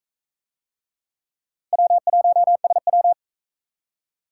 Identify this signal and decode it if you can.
Morse code, heard as a keyed tone, about 44 wpm, 700 Hz. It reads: W1SW